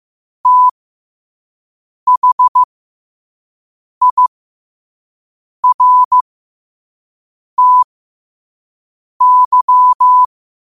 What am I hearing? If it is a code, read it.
Morse code, heard as a keyed tone, about 15 wpm, spaced as for 5 wpm, 1000 Hz. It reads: THIRTY
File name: qotdfiles / 5wpm